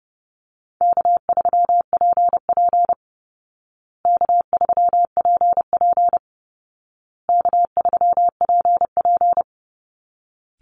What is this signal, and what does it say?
Morse code, heard as a keyed tone, about 30 wpm, 700 Hz. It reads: K3PP K3PP K3PP
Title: Easy Call Signs Repeated 3x - 30wpm